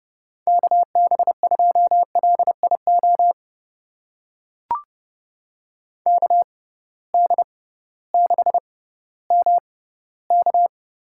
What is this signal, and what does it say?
Morse code, heard as a keyed tone, about 32 wpm, spaced as for 17 wpm, 700 Hz. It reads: KB2LIO KD6MK
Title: USA Call Signs - 6 Characters - 6x Character Spacing - 30wpm